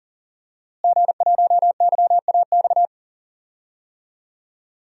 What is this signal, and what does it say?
Morse code, heard as a keyed tone, about 37 wpm, 700 Hz. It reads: G1YAX
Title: DXCC-100 Call Signs - 5 Characters - Repeated 3x - 40wpm